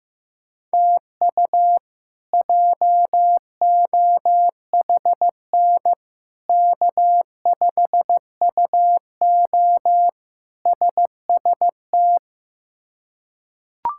Morse code, heard as a keyed tone, about 15 wpm, 700 Hz. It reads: TU JOHN K5UO SST
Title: K1USN SST QSO Practice - 15wpm